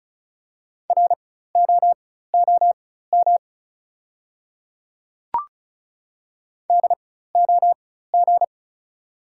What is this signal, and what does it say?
Morse code, heard as a keyed tone, about 34 wpm, spaced as for 16 wpm, 700 Hz. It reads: ROOM DOG
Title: 4x Character Spacing - Top 500 Words - 35wpm